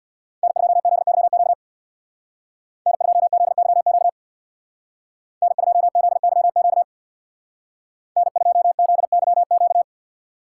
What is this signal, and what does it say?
Morse code, heard as a keyed tone, about 62 wpm, 700 Hz. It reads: N2ZYQ N2ZYQ N2ZYQ N2ZYQ
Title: Speed Racing - USA 1x3 Call Signs 50wpm